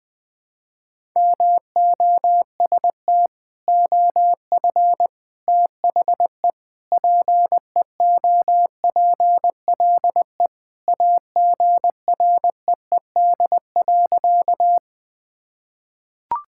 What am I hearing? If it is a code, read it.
Morse code, heard as a keyed tone, about 20 wpm, 700 Hz. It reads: MOST OF THE PEOPLE AGREED.
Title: Sentences with 3-Word Phrases - 20wpm